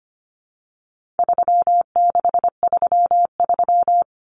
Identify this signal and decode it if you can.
Morse code, heard as a keyed tone, about 25 wpm, 700 Hz. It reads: 3633
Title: Common Cut Numbers - 4-Digit 25wpm